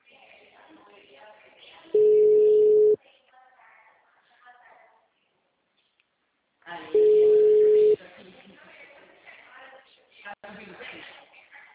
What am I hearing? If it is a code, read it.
Ringback tone (Europe, 425 Hz), with speech.
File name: out- -2002- - - .16813